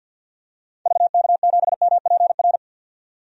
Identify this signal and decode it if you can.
Morse code, heard as a keyed tone, about 52 wpm, 700 Hz. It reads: VK7MPR
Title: DXCC-100 Call Signs - 6 Characters - 50wpm